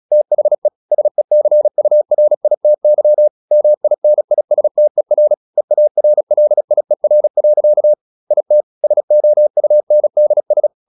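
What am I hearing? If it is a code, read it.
Morse code, heard as a keyed tone, about 36 wpm, 600 Hz. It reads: THE SECURITY MINISTER EARLIER. IT SOUNDS
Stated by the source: Morse practice at 36 wpm, text of Ken Follett's novel Never, from CW med Gnister.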